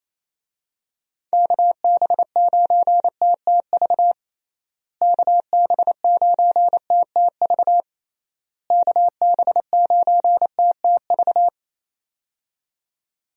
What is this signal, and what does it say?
Morse code, heard as a keyed tone, about 28 wpm, 700 Hz. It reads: KB9TTV KB9TTV KB9TTV
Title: USA Call Signs - 6 Characters - Repeated 3x - 28wpm